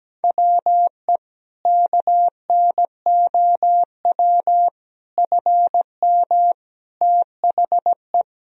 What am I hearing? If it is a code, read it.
Morse code, heard as a keyed tone, about 17 wpm, 700 Hz. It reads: WE KNOW FM THE
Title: Sentences from Top 500 Words and Abbreviations 17wpm